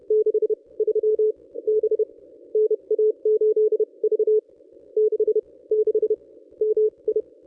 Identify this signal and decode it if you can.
Morse code, heard as a keyed tone, about 30 wpm, spaced as for 23 wpm, 430 Hz. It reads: 63 B NA8V 66 MI
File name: contests / ss-cw-k5tr-op-n5rz -1000